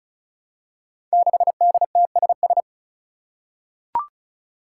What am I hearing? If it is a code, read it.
Morse code, heard as a keyed tone, about 36 wpm, 700 Hz. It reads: 6DTSS
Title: Five Challenging Letter-Number Combinations 35wpm